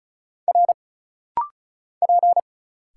Morse code, heard as a keyed tone, about 34 wpm, 700 Hz. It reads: R P
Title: Single Letters - Rapid-Fire 35wpm